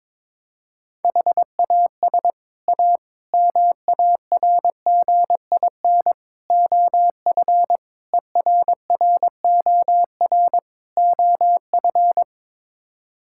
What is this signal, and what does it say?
Morse code, heard as a keyed tone, about 22 wpm, 700 Hz. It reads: HAS A MARGIN OF ERROR OF
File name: practice / 22wpm / Sets of 6 Words - 22wpm